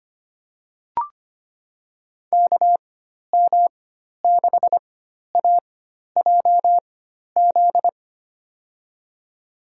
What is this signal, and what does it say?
Morse code, heard as a keyed tone, about 25 wpm, spaced as for 12 wpm, 700 Hz. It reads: KM6AJZ
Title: USA Call Signs - 6 Characters - 4x Character Spacing - 25wpm